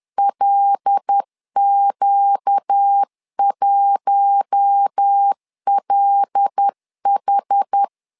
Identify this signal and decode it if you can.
Morse code, heard as a keyed tone, about 10 wpm, 800 Hz. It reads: LQ1LH
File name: LEZIONI / 5 LBVC / 5 L 10wpm 2259